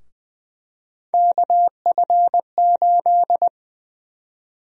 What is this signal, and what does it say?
Morse code, heard as a keyed tone, about 20 wpm, 700 Hz. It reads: KF8
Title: USA Call Sign Prefixes - Repeated 2x - 20wpm